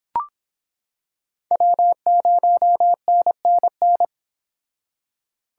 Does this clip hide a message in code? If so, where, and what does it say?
Yes: Morse code from 1.5 to 4.0 s, heard as a keyed tone, about 26 wpm, 700 Hz. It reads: W0NNN